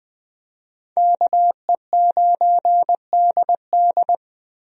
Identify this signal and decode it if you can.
Morse code, heard as a keyed tone, about 20 wpm, 700 Hz. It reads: KE9DD
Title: USA Call Signs - 5 Characters - Repeated 3x - 20wpm